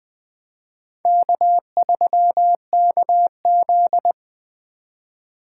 Morse code, heard as a keyed tone, about 21 wpm, 700 Hz. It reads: K3KZ
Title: Speed Racing - USA Call Signs 17wpm